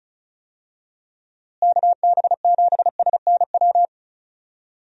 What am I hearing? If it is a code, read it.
Morse code, heard as a keyed tone, about 34 wpm, 700 Hz. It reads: KB7SNW
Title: USA Call Signs - 6 Characters - Repeated 3x - 35wpm